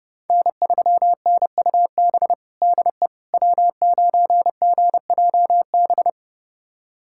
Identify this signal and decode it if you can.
Morse code, heard as a keyed tone, about 30 wpm, 700 Hz. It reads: N3NUB DE W9GJB
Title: USA Callsign DE Callsign - 5 Characters - 30wpm